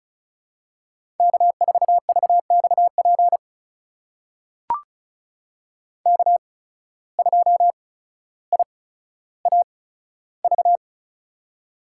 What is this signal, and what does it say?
Morse code, heard as a keyed tone, about 36 wpm, spaced as for 15 wpm, 700 Hz. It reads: K4VXP K2IAV
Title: USA Call Signs - 5 Characters - 8x Character Spacing - 35wpm